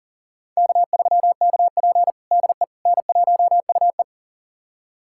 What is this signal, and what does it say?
Morse code, heard as a keyed tone, about 40 wpm, 700 Hz. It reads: K3KP DE N1UE
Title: USA Callsign DE Callsign - 4 Characters - 40wpm